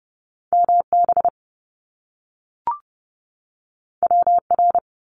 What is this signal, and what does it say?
Morse code, heard as a keyed tone, about 30 wpm, 700 Hz. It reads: MB WR
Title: Top Two Letter Combinations in Words 30wpm